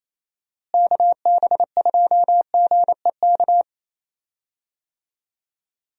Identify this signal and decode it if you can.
Morse code, heard as a keyed tone, about 28 wpm, 700 Hz. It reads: KB2GEK
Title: USA Call Signs - 6 Characters - Repeated 2x - 28wpm